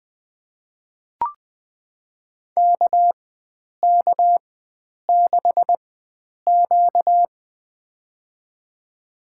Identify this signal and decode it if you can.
Morse code, heard as a keyed tone, about 20 wpm, spaced as for 9 wpm, 700 Hz. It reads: KK6Q
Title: USA Call Signs - 4 Characters - 4x Character Spacing - 20wpm